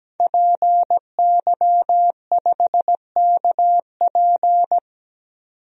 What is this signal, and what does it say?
Morse code, heard as a keyed tone, about 17 wpm, 700 Hz. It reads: PY5KP
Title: DXCC-100 Call Signs - 5 Characters - 17wpm